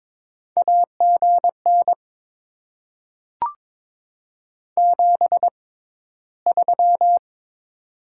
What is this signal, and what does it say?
Morse code, heard as a keyed tone, about 21 wpm, spaced as for 13 wpm, 700 Hz. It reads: AGN 73
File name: Top 20 QSO Elements - 6x Character Spacing - 22wpm